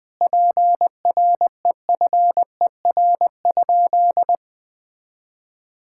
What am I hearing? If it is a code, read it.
Morse code, heard as a keyed tone, about 20 wpm, 700 Hz. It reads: PREFER?
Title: Sentences from Top 1600 Words 20wpm